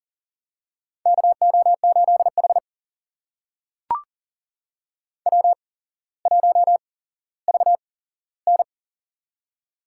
Morse code, heard as a keyed tone, about 41 wpm, spaced as for 17 wpm, 700 Hz. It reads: KO8H W1VN
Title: USA Call Signs - 4 Characters - 8x Character Spacing - 40wpm